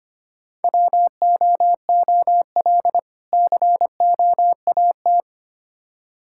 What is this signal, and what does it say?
Morse code, heard as a keyed tone, about 25 wpm, 700 Hz. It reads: WOOL COAT
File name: practice / 25wpm / Difficult Words with Hints - 25wpm